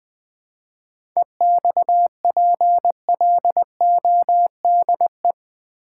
Morse code, heard as a keyed tone, about 20 wpm, 700 Hz. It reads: EXPLODE
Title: 7 Letter Words - First 3 Letters Given - 20wpm